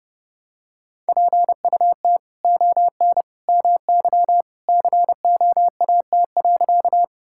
Morse code, heard as a keyed tone, about 30 wpm, 700 Hz. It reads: PUT ON MY COAT.
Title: Sentences from Top 1600 Words 30wpm